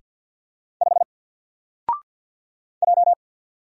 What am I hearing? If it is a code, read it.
Morse code, heard as a keyed tone, about 50 wpm, 700 Hz. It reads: 5 J
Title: Single Letter-Number Alt 50wpm